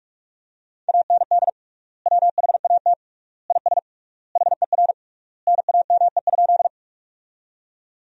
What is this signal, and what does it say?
Morse code, heard as a keyed tone, about 41 wpm, 700 Hz. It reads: AND WHAT IS HER NAME?
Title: 3x Word-Spacing - Sentences from Top 200 Words - 45wpm